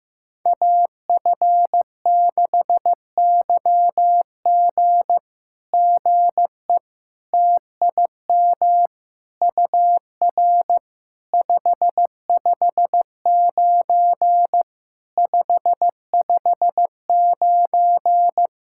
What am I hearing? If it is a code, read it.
Morse code, heard as a keyed tone, about 15 wpm, 700 Hz. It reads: AF6YG GE TIM UR 559 559
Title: SOTA QSO Practice 15wpm